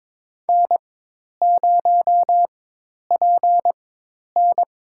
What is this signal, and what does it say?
Morse code, heard as a keyed tone, about 22 wpm, spaced as for 10 wpm, 700 Hz. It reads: N0PN
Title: USA Call Signs - 4 Characters - 4x Character Spacing - 22wpm